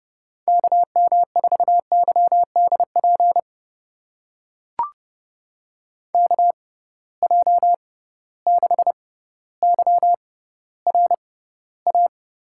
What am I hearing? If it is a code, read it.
Morse code, heard as a keyed tone, about 31 wpm, spaced as for 15 wpm, 700 Hz. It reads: KM4YDP KJ6YRA
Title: USA Call Signs - 6 Characters - 6x Character Spacing - 30wpm